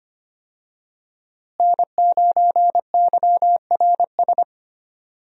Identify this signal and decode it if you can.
Morse code, heard as a keyed tone, about 25 wpm, 700 Hz. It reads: N9YRS
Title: USA Call Signs - 5 Characters - Repeated 3x - 25wpm